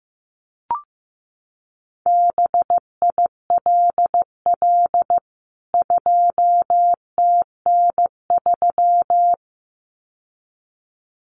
Morse code, heard as a keyed tone, about 15 wpm, 700 Hz. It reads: BILL 2TN3
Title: CWT Exchange 15wpm